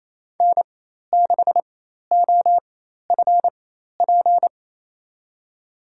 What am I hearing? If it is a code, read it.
Morse code, heard as a keyed tone, about 28 wpm, spaced as for 13 wpm, 700 Hz. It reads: N6OFP
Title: USA Call Signs - 5 Characters - 4x Character Spacing - 28wpm